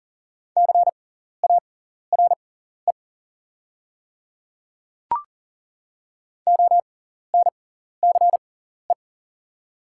Morse code, heard as a keyed tone, about 40 wpm, spaced as for 14 wpm, 700 Hz. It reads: CARE ONCE
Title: 6x Character Spacing - Top 300 Words - 40wpm